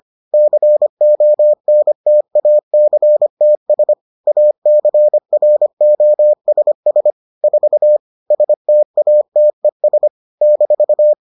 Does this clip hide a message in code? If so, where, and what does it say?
Yes: Morse code from 0.3 to 11.1 s, heard as a keyed tone, about 25 wpm, 600 Hz. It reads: CONTACTS ACROSS 4 STATES -